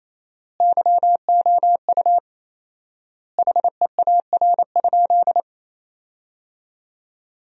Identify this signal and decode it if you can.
Morse code, heard as a keyed tone, about 28 wpm, 700 Hz. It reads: YOU HEAR?
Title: 4x Word-Spacing - Sentences from Top 300 Words - 28wpm